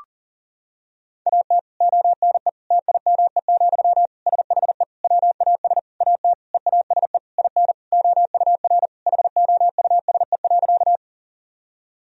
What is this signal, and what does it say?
Morse code, heard as a keyed tone, about 40 wpm, 700 Hz. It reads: AT ONE TIME, SHE WAS AT EASE IN OUR HOUSE.
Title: Sentences from Top 1000 Words - Review Part 1 - 40wpm